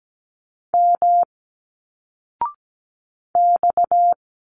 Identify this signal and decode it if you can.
Morse code, heard as a keyed tone, about 17 wpm, 700 Hz. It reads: M X
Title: Single Letters 17wpm